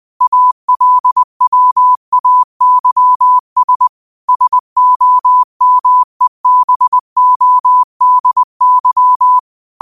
Morse code, heard as a keyed tone, about 20 wpm, 1000 Hz. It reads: ALWAYS SOMEBODY
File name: qotdfiles / 20wpm